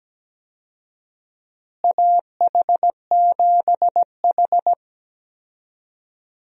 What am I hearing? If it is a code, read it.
Morse code, heard as a keyed tone, about 17 wpm, 700 Hz. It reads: AH7H
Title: USA Call Signs - 4 Characters - Repeated 3x - 17wpm